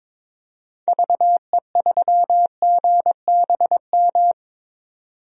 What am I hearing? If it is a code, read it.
Morse code, heard as a keyed tone, about 22 wpm, 700 Hz. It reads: VE3GBM
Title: DXCC-100 Call Signs - 6 Characters - 22wpm